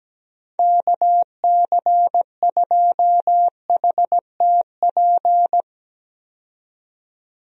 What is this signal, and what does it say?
Morse code, heard as a keyed tone, about 17 wpm, 700 Hz. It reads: KC2HTP
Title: USA Call Signs - 6 Characters - 17wpm